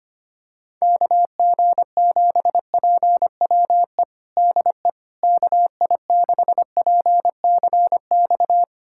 Morse code, heard as a keyed tone, about 25 wpm, 700 Hz. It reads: KG7PWE DE KI6PCX
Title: USA Callsign DE Callsign - 6 Characters - 25wpm